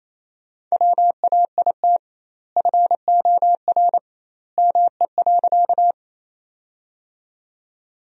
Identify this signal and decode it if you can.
Morse code, heard as a keyed tone, about 28 wpm, 700 Hz. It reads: WAIT FOR ME.
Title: 2x Word-Spacing - Sentences from Top 500 Words - 28wpm